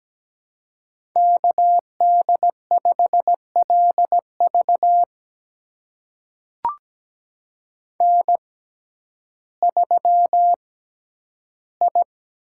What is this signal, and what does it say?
Morse code, heard as a keyed tone, about 17 wpm, spaced as for 10 wpm, 700 Hz. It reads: KD5LV N3I
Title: USA Call Signs - 5 Characters - 6x Character Spacing - 17wpm